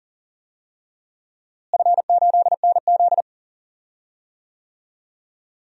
Morse code, heard as a keyed tone, about 40 wpm, 700 Hz. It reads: F8NZ
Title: DXCC-100 Call Signs - 4 Characters - Repeated 2x - 40wpm